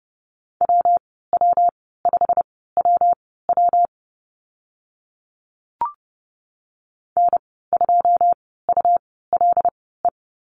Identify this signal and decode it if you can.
Morse code, heard as a keyed tone, about 30 wpm, spaced as for 17 wpm, 700 Hz. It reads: WW5WW N2ULE
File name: Lesson 19g - Letter w - Call Signs 30wpm